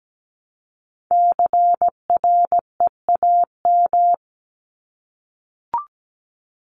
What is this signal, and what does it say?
Morse code, heard as a keyed tone, about 17 wpm, 700 Hz. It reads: CREAM
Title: Top 1251-1500 Words 17wpm